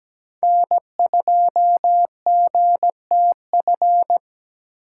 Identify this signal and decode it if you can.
Morse code, heard as a keyed tone, about 17 wpm, 700 Hz. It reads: N2GTF
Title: USA Call Signs - 5 Characters - 8x Character Spacing - 17wpm